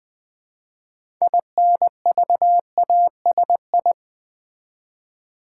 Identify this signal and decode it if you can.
Morse code, heard as a keyed tone, about 20 wpm, 700 Hz. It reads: INVASI
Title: Build A Word - 3 Letters At A Time - 8-Letter Words - 20wpm